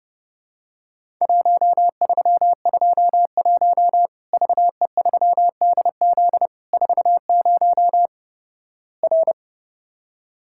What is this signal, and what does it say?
Morse code, heard as a keyed tone, about 29 wpm, 700 Hz. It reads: 1321 VE3DZ 40 R
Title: Worked All Europe DX Contest - QTC Exchange - 30wpm